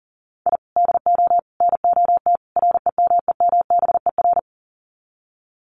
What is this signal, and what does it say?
Morse code, heard as a keyed tone, about 40 wpm, 700 Hz. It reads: I DO NOT REMEMBER
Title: Sets of 4 Words 40wpm